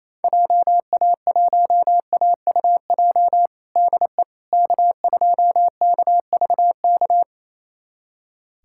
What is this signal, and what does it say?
Morse code, heard as a keyed tone, about 27 wpm, 700 Hz. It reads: JA1AUJ DE K2KVK
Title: USA Callsign DE Callsign 28wpm